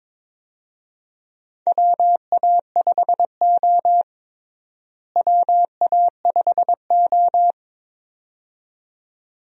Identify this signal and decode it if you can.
Morse code, heard as a keyed tone, about 22 wpm, 700 Hz. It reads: WA5O WA5O
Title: USA Call Signs - 4 Characters - Repeated 2x - 22wpm